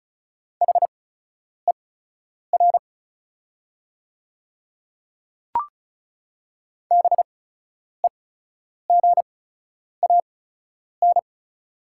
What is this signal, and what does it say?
Morse code, heard as a keyed tone, about 34 wpm, spaced as for 10 wpm, 690 Hz. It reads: HER BEGAN
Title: 8x Character Spacing - Top 500 Words - 35wpm